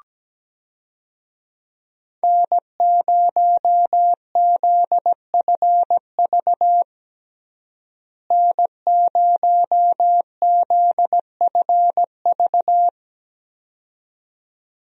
Morse code, heard as a keyed tone, about 17 wpm, 700 Hz. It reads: N0ZFV N0ZFV
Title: USA Call Signs - 5 Characters - Repeated 2x - 17wpm